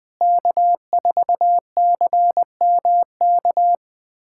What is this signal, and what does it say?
Morse code, heard as a keyed tone, about 20 wpm, 700 Hz. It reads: K4CMK
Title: USA Callsign DE Callsign - 5 Characters - 20wpm